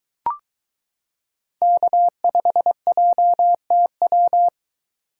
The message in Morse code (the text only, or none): K5JTW